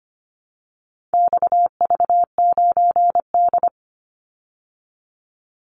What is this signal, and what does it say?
Morse code, heard as a keyed tone, about 25 wpm, 700 Hz. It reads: XV9D
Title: Call Signs Repeated 3x - 25wpm